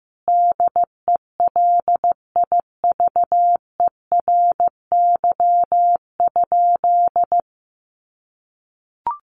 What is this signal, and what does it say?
Morse code, heard as a keyed tone, about 15 wpm, 700 Hz. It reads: DELIVERY?
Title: Sentences from Top 1500 Words 15wpm